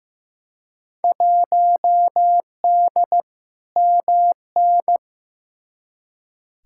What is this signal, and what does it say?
Morse code, heard as a keyed tone, about 15 wpm, 700 Hz. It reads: 1D MN
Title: Field Day Exchanges - 15wpm